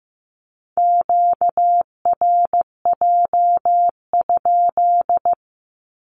Morse code, heard as a keyed tone, about 15 wpm, 700 Hz. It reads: QRJ?
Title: Q-Codes with Spoken Meaning - 15wpm